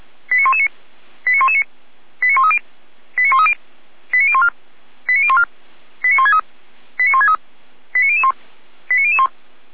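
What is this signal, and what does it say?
Five-tone selective call (ZVEI), heard as quick runs of five tones, tones of about 80 ms. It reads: 89119 89119 89130 89130 89014 89014 89163 89163 89001 89001
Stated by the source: ZVEI five-tone selective call received with an SDR.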